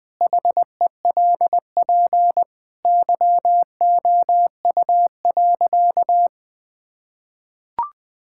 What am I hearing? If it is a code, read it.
Morse code, heard as a keyed tone, about 20 wpm, 700 Hz. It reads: HELP YOU.
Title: Sentences with 4-Word Phrases - Spoken Beginning - 20wpm